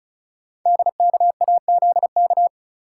Morse code, heard as a keyed tone, about 35 wpm, 690 Hz. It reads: DKAZK
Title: Code Groups - 5 Letters - 35wpm